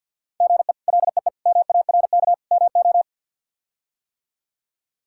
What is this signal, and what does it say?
Morse code, heard as a keyed tone, about 49 wpm, 700 Hz. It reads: GE LEE MARK MO
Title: K1USN SST QSO Practice - 50wpm